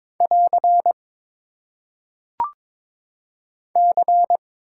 Morse code, heard as a keyed tone, about 22 wpm, 700 Hz. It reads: <AR> C
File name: Top 90 QSO Elements - 8x Character Spacing - 22wpm